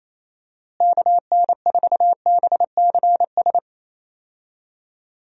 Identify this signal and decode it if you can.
Morse code, heard as a keyed tone, about 27 wpm, 700 Hz. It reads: KN4BCS
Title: USA Call Signs - 6 Characters - Repeated 2x - 28wpm